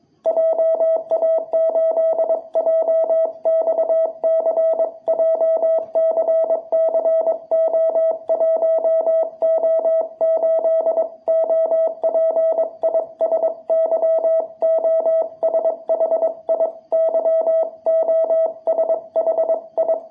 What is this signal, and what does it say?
Morse code, heard as a keyed tone, about 22 wpm, 660 Hz. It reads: JA8JXC JCCO1O8 OP IS YOSHI YOSHI
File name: jh8cbh / 2021cw